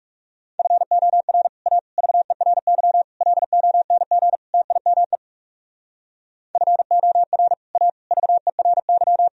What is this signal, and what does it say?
Morse code, heard as a keyed tone, about 38 wpm, 700 Hz. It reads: FOR A VERY LONG TIME FOR A VERY